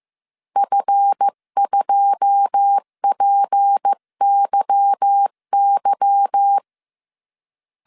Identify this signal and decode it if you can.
Morse code, heard as a keyed tone, about 15 wpm, 800 Hz. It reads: F2PYY
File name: CW / DET / 1 9 1500C S B 14wpm 3302